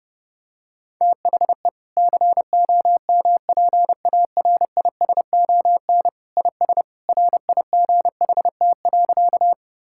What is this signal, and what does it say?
Morse code, heard as a keyed tone, about 32 wpm, 700 Hz. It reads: THE COMPARISON IS RIGHT.
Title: Sentences from Top 1600 Words 30wpm